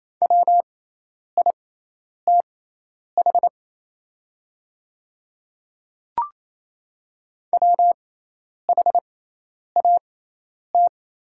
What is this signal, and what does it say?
Morse code, heard as a keyed tone, about 28 wpm, spaced as for 10 wpm, 700 Hz. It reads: WITH WHAT
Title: 6x Character Spacing - Top 200 Words - 28wpm